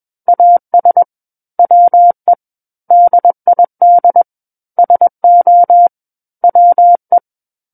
Morse code, heard as a keyed tone, about 21 wpm, 700 Hz. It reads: AS WE DID SO WE
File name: Book One / 12-Chapter 12 - 21 WPM